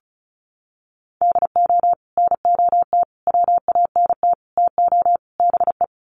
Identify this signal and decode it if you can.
Morse code, heard as a keyed tone, about 34 wpm, 700 Hz. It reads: DO NOT WANT TO BE
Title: Sets of 5 Words 35wpm